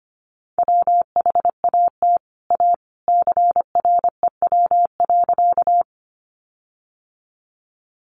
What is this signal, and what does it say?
Morse code, heard as a keyed tone, about 25 wpm, 700 Hz. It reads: WHAT A CREW.
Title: Sentences from Top 1500 Words 25wpm